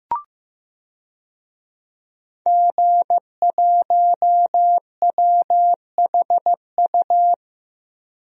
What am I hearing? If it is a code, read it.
Morse code, heard as a keyed tone, about 15 wpm, 700 Hz. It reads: G1WHU
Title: DXCC-100 Call Signs - 5 Characters - 15wpm